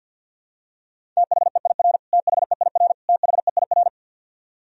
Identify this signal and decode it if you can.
Morse code, heard as a keyed tone, about 51 wpm, 700 Hz. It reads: THEIR THEIR THEIR
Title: Top 100 Words Repeated 50wpm